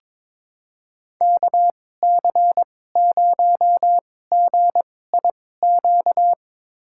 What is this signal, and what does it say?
Morse code, heard as a keyed tone, about 22 wpm, spaced as for 16 wpm, 700 Hz. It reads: KC0GIQ
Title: USA Call Signs - 6 Characters - 2x Character Spacing - 22wpm